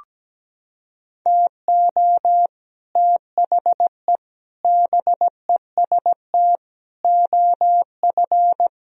Morse code, heard as a keyed tone, about 17 wpm, 700 Hz. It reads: TO THE BEST OF
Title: Sentences from Top 1000 Words - Review Part 3 - 17wpm